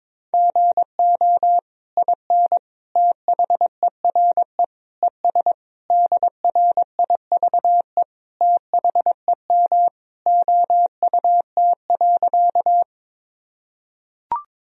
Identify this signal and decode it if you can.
Morse code, heard as a keyed tone, about 22 wpm, 700 Hz. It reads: GO IN THERE ES DRIVE THEM OUT.
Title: Sentences from Top 500 Words and Abbreviations 22wpm